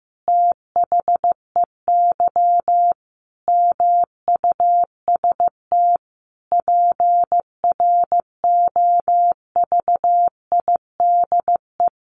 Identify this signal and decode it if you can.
Morse code, heard as a keyed tone, about 15 wpm, 700 Hz. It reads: THEY MUST PROVIDE
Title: Sentences from Top 1100 Words 15wpm